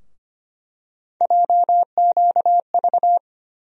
Morse code, heard as a keyed tone, about 25 wpm, 700 Hz. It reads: JQV
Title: USA Call Sign Postfixes - 3 Letters 25wpm